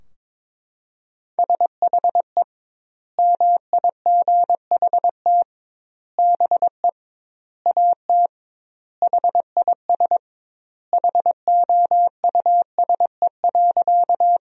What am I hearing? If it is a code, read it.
Morse code, heard as a keyed tone, about 22 wpm, 700 Hz. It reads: SHE MIGHT BE AT HIS HOUSE.
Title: 2x Word-Spacing - Sentences from Top 300 Words - 22wpm